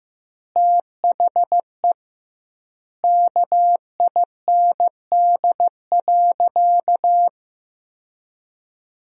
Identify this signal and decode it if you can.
Morse code, heard as a keyed tone, about 16 wpm, 700 Hz. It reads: THE KIND.
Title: 2x Word-Spacing - Sentences from Top 500 Words - 15wpm